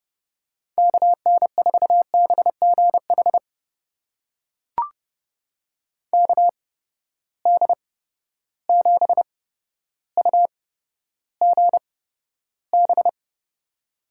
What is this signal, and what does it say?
Morse code, heard as a keyed tone, about 30 wpm, spaced as for 13 wpm, 700 Hz. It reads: KN4BGH KD7UGB
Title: USA Call Signs - 6 Characters - 8x Character Spacing - 30wpm